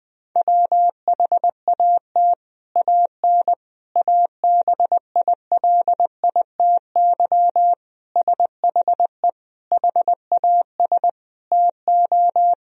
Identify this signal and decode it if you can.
Morse code, heard as a keyed tone, about 21 wpm, 700 Hz. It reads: WHAT AN ABILITY SHE HAS TO
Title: Sentences from Top 1000 Words - Review Part 3 - 20wpm